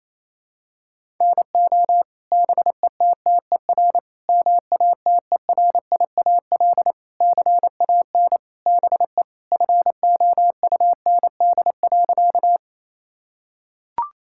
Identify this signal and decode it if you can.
Morse code, heard as a keyed tone, about 28 wpm, 700 Hz. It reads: NO BETTER MATERIAL CAN BE FOUND.
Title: Sentences from Top 1000 Words - Review Part 5 - 28wpm